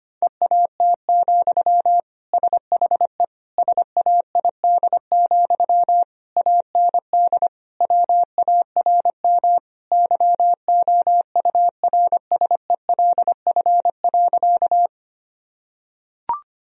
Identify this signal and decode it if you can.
Morse code, heard as a keyed tone, about 25 wpm, 700 Hz. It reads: EAT, SHE SAID, AND WARM YOURSELF.
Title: Sentences from Top 1000 Words - Review Part 2 - 25wpm